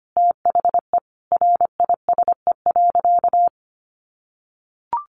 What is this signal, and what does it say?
Morse code, heard as a keyed tone, about 25 wpm, 700 Hz. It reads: THE RISE.
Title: Sentences from Top 1300 Words 25wpm